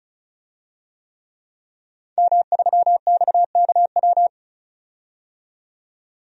Morse code, heard as a keyed tone, about 34 wpm, 700 Hz. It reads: M3XKW
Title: DXCC-100 Call Signs - 5 Characters - 35wpm